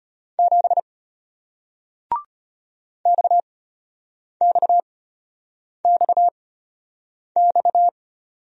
Morse code, heard as a keyed tone, about 34 wpm, spaced as for 8 wpm, 690 Hz. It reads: 7 XXXX